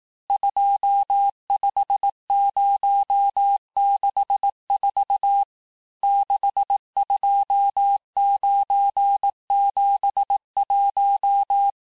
Morse code, heard as a keyed tone, about 18 wpm, 800 Hz. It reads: 25064 62971
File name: cours-du-lundi files / cours-1594--teste-chiffre-vitesse-17mpm